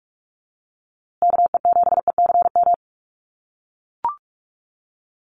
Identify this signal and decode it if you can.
Morse code, heard as a keyed tone, about 44 wpm, 700 Hz. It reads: KE7ECM
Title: USA Call Signs Repeated 2x 45wpm